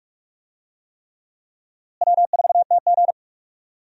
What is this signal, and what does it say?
Morse code, heard as a keyed tone, about 42 wpm, 700 Hz. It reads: W4TG
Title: USA Call Signs - 4 Characters - Repeated 3x - 45wpm